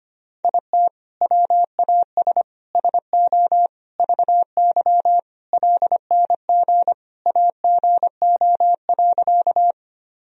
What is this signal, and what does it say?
Morse code, heard as a keyed tone, about 25 wpm, 700 Hz. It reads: IT WAS SO VY LNG AGO.
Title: Sentences from Top 500 Words and Abbreviations 25wpm